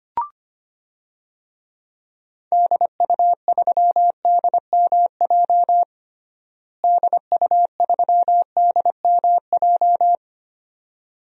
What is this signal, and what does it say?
Morse code, heard as a keyed tone, about 25 wpm, 700 Hz. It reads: DU3DMJ DU3DMJ
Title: DXCC-100 Call Signs - 6 Characters - Repeated 2x - 25wpm